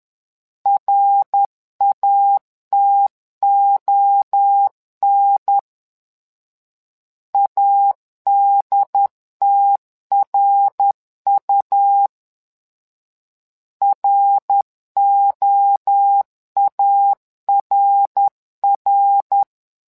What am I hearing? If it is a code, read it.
Morse code, heard as a keyed tone, about 10 wpm, 800 Hz. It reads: RATON ADTRU ROARR
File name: LEZIONI / 3 MWRO / 3 MWRO 10wpm 2148